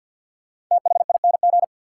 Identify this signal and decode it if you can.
Morse code, heard as a keyed tone, about 52 wpm, 700 Hz. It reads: THING
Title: Sets of 3 Words - New Word Drill - 50wpm